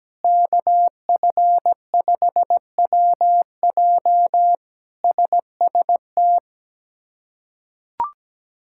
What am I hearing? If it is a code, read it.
Morse code, heard as a keyed tone, about 17 wpm, 700 Hz. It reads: KF5WJ SST
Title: K1USN SST QSO Practice - 17wpm